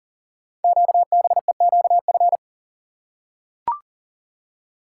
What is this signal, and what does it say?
Morse code, heard as a keyed tone, about 42 wpm, 700 Hz. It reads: QBEQF
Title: Code Groups - 5 Letters - 40wpm